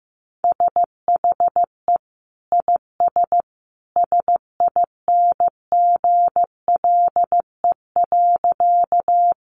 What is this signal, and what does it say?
Morse code, heard as a keyed tone, about 15 wpm, 700 Hz. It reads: SHE IS SINGLE.
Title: Sentences from Top 700 Words 15wpm